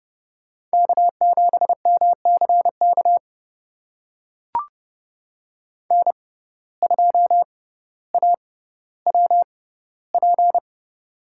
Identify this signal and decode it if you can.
Morse code, heard as a keyed tone, about 30 wpm, spaced as for 15 wpm, 700 Hz. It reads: K7MCK N2AWP
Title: USA Call Signs - 5 Characters - 6x Character Spacing - 30wpm